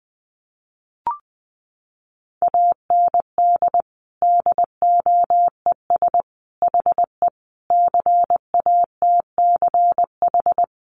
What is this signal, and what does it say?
Morse code, heard as a keyed tone, about 20 wpm, 700 Hz. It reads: AND DOES HE CATCH
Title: Sentences from Top 1500 Words 20wpm